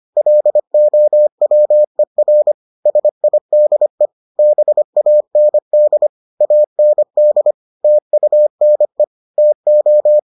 Morse code, heard as a keyed tone, about 25 wpm, 600 Hz. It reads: LOWER SIDE BAND AND TUNE TO